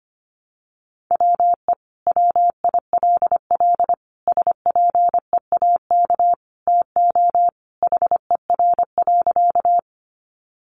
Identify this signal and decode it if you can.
Morse code, heard as a keyed tone, about 25 wpm, 700 Hz. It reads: WE WILL SPEAK TO HER.
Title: Sentences from Top 1100 Words 25wpm